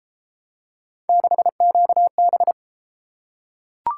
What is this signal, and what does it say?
Morse code, heard as a keyed tone, about 33 wpm, 700 Hz. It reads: 6QB